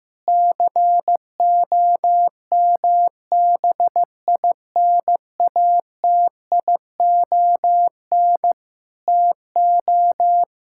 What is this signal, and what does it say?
Morse code, heard as a keyed tone, about 15 wpm, 700 Hz. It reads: COMBINATION TO
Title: Sentences from Top 1000 Words - Review Part 6 - 15wpm